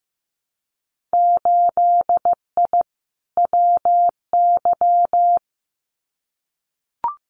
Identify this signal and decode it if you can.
Morse code, heard as a keyed tone, about 15 wpm, 700 Hz. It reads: 8I WY
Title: Winter Field Day Exchanges Encore 15wpm